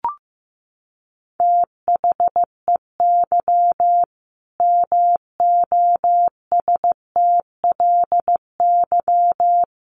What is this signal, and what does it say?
Morse code, heard as a keyed tone, about 15 wpm, 700 Hz. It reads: THEY MOSTLY
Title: Sentences from Top 1300 Words 15wpm